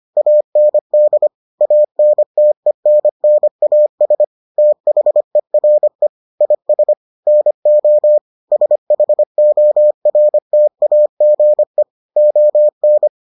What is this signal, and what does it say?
Morse code, heard as a keyed tone, about 25 wpm, 600 Hz. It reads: AND ANTENNAS THERE IS NO SHORTAGE ON